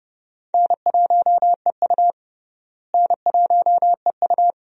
Morse code, heard as a keyed tone, about 29 wpm, 700 Hz. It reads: N1EU N1EU
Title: USA Call Signs - 4 Characters - Repeated 2x - 30wpm